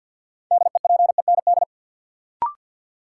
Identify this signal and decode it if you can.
Morse code, heard as a keyed tone, about 53 wpm, 700 Hz. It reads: DEPEND
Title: Sentences from Top 1300 Words - New Word Drill Alt - 50wpm